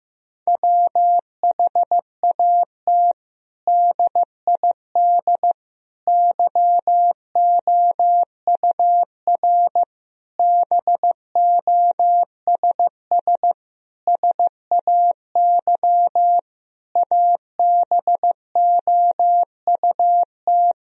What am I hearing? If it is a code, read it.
Morse code, heard as a keyed tone, about 15 wpm, 700 Hz. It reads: WHAT DID YOUR BOSS SAY ABOUT
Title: Sentences from Top 1000 Words - Review Part 4 - 15wpm